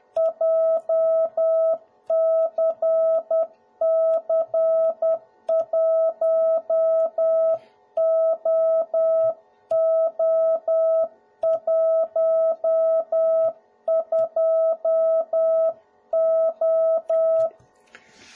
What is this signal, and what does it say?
Morse code, heard as a keyed tone, about 10 wpm, 650 Hz. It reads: JCC1OO12O
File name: jh8cbh / 2021cw / cw01b